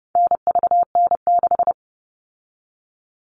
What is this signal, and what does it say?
Morse code, heard as a keyed tone, about 30 wpm, 700 Hz. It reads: NVN6
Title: All Cut Numbers - 4-Digit 30wpm